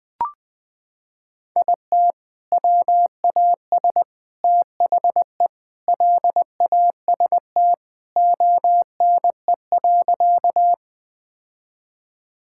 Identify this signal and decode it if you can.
Morse code, heard as a keyed tone, about 21 wpm, 700 Hz. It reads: IT WAS THE LAST ONE.
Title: Sentences with 3-Word Phrases - 20wpm